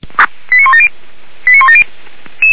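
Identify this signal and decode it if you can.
Five-tone selective call (ZVEI), heard as quick runs of five tones, tones of about 70 ms. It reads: 89170 89170
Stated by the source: Five-tone ZVEI selective call, SDR-received, address 89185.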